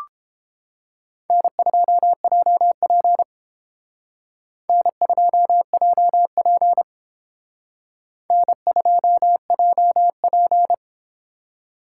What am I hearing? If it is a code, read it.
Morse code, heard as a keyed tone, about 29 wpm, 700 Hz. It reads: N2JP N2JP N2JP